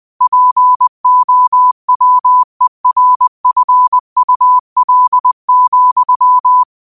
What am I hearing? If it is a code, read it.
Morse code, heard as a keyed tone, about 20 wpm, 1000 Hz. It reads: POWERFUL,
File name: qotdfiles / 20wpm